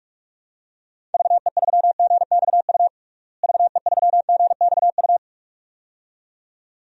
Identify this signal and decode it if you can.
Morse code, heard as a keyed tone, about 46 wpm, 700 Hz. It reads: VE3GXU VE3GXU
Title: DXCC-100 Call Signs - 6 Characters - Repeated 2x - 45wpm